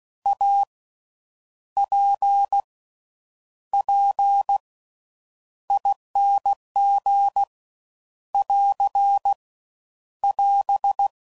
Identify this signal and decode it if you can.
Morse code, heard as a keyed tone, about 16 wpm, spaced as for 8 wpm, 800 Hz. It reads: APPING <AR> <AS>
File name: Slow With Faster Ings